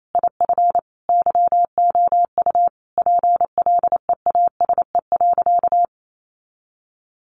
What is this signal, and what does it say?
Morse code, heard as a keyed tone, about 28 wpm, 700 Hz. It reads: IF YOU PLEASE.
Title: Sentences from Top 1500 Words 28wpm